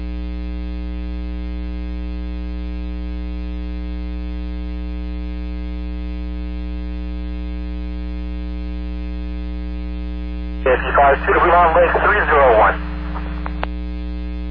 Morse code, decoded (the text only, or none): none